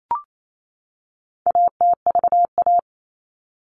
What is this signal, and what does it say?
Morse code, heard as a keyed tone, about 28 wpm, 700 Hz. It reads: ATVA
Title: All Cut Numbers - 4-Digit 28wpm